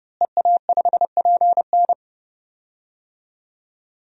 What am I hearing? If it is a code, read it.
Morse code, heard as a keyed tone, about 30 wpm, 700 Hz. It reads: EA5PN
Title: DXCC-100 Call Signs - 5 Characters - Repeated 2x - 30wpm